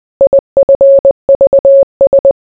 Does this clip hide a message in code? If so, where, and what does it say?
Yes: Morse code from 0.2 to 2.3 s, heard as a keyed tone, about 20 wpm, 550 Hz. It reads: IFVS